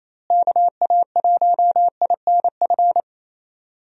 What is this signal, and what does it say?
Morse code, heard as a keyed tone, about 28 wpm, 700 Hz. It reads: KA1INF
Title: USA Call Signs - 6 Characters - Repeated 3x - 28wpm